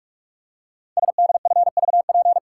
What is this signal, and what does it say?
Morse code, heard as a keyed tone, about 44 wpm, 700 Hz. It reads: SDFVP
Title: Code Groups - 5 Letter:Numbers - 45wpm